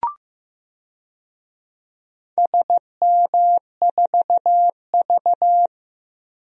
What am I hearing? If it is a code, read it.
Morse code, heard as a keyed tone, about 15 wpm, 700 Hz. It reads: SM4V
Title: DXCC-100 Call Signs - 4 Characters - 15wpm